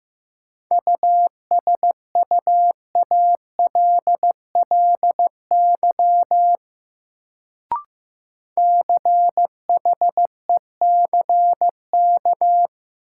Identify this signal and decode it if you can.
Morse code, heard as a keyed tone, about 15 wpm, 700 Hz. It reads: USUALLY CHECK
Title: Sentences from Top 1000 Words - Review Part 3 - New Word Drill - 15wpm